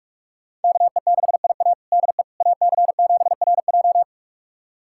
Morse code, heard as a keyed tone, about 46 wpm, 700 Hz. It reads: KE6IA DE AC7RJ
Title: USA Callsign DE Callsign - 5 Characters - 45wpm